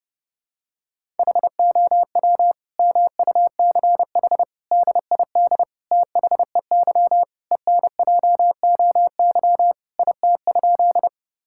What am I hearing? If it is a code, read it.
Morse code, heard as a keyed tone, about 30 wpm, 700 Hz. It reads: HOW MUCH DID THEY ENJOY IT?
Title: Sentences from Top 1000 Words - Review Part 5 - 30wpm